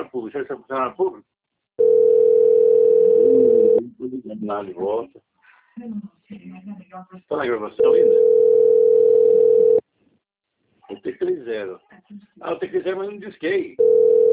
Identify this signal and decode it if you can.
Ringback tone (North America), 440 + 480 Hz, with speech.
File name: rg-1600-unknown- - - .43285